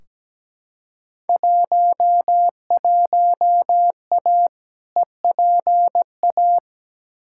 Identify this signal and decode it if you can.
Morse code, heard as a keyed tone, about 17 wpm, 700 Hz. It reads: 11A EPA
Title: Field Day Exchanges Encore - 17wpm